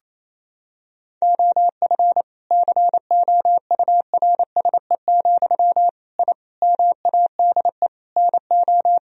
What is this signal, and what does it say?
Morse code, heard as a keyed tone, about 28 wpm, 700 Hz. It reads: OF COURSE, I MADE NO
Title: Sentences from Top 1000 Words - Review Part 1 - 28wpm